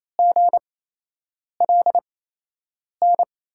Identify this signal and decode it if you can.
Morse code, heard as a keyed tone, about 28 wpm, spaced as for 8 wpm, 700 Hz. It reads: GLN
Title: USA Call Signs - 8x Character Spacing - 28wpm